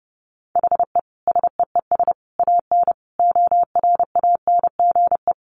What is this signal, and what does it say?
Morse code, heard as a keyed tone, about 30 wpm, 700 Hz. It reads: HE SEES AN ORANGE
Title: Easy Phrases 30wpm